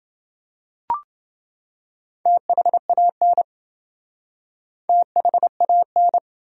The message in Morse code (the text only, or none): THAN THAN